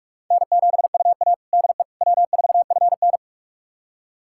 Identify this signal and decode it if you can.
Morse code, heard as a keyed tone, about 41 wpm, 700 Hz. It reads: N7UA DE W4FN
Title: USA Callsign DE Callsign - 4 Characters - 45wpm